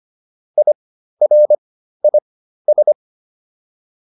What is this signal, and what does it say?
Morse code, heard as a keyed tone, about 25 wpm, spaced as for 13 wpm, 600 Hz. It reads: IRIS